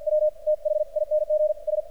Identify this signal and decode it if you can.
Morse code, heard as a keyed tone, about 46 wpm, spaced as for 40 wpm, 610 Hz. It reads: W THINGS
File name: archived / CRSnet